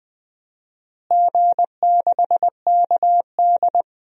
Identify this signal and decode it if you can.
Morse code, heard as a keyed tone, about 21 wpm, 700 Hz. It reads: G6KD
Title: DXCC-100 Call Signs - 4 Characters - Repeated 3x - 20wpm